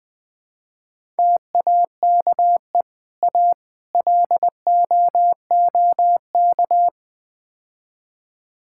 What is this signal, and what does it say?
Morse code, heard as a keyed tone, about 21 wpm, 700 Hz. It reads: TAKE A LOOK
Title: Build A Phrase - 5-Word Phrases - 20wpm